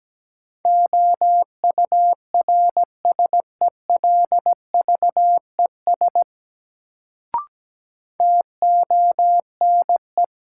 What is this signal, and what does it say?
Morse code, heard as a keyed tone, about 17 wpm, 700 Hz. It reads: OURSELVES TONE
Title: Sentences from Top 1600 Words - New Word Drill - 17wpm